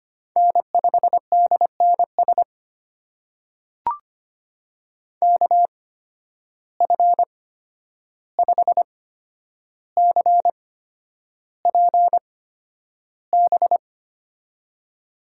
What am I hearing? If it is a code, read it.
Morse code, heard as a keyed tone, about 25 wpm, spaced as for 10 wpm, 700 Hz. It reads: N5DNS KF5CPB
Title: USA Call Signs - 8x Character Spacing - 25wpm